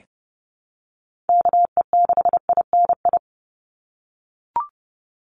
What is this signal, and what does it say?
Morse code, heard as a keyed tone, about 30 wpm, 700 Hz. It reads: KE6INI
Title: USA Call Signs 30wpm